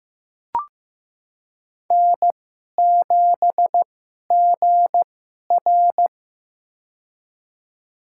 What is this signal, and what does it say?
Morse code, heard as a keyed tone, about 15 wpm, spaced as for 11 wpm, 700 Hz. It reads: N7GR
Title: USA Call Signs - 4 Characters - 2x Character Spacing - 15wpm